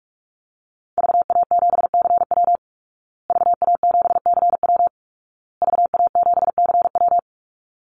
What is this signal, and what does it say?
Morse code, heard as a keyed tone, about 44 wpm, 700 Hz. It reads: VA7CW VA7CW VA7CW
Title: Call Signs Repeated 2x - 45wpm